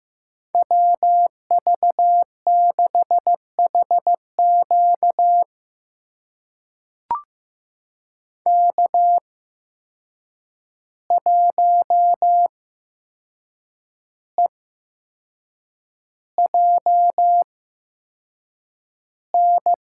Morse code, heard as a keyed tone, about 15 wpm, spaced as for 6 wpm, 700 Hz. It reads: WV6HQ K1EJN
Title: USA Call Signs - 5 Characters - 8x Character Spacing - 15wpm